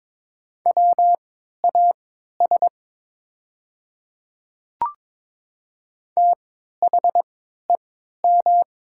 Morse code, heard as a keyed tone, about 22 wpm, spaced as for 12 wpm, 700 Hz. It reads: WAS THEM
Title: 3x Character Spacing - Top 200 Words - 22wpm